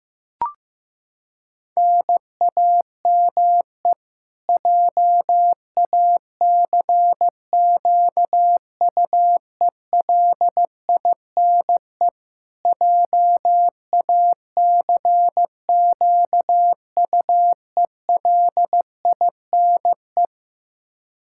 Morse code, heard as a keyed tone, about 15 wpm, 700 Hz. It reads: NAME JACQUELINE JACQUELINE
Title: Practice QSO - Exchange 1 - Beginning - 15wpm